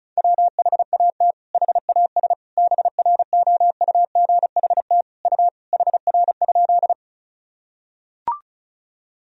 Morse code, heard as a keyed tone, about 36 wpm, 700 Hz. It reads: WHAT HAS BROUGHT U HR?
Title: Sentences from Top 500 Words and Abbreviations 35wpm